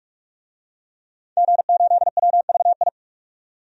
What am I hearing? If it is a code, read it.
Morse code, heard as a keyed tone, about 42 wpm, 700 Hz. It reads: G8WVI
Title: DXCC-100 Call Signs - 5 Characters - 45wpm